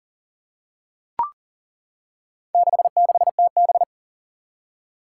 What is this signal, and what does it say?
Morse code, heard as a keyed tone, about 40 wpm, 690 Hz. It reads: 66TB
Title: Four Challenging Letter-Number Combinations 40wpm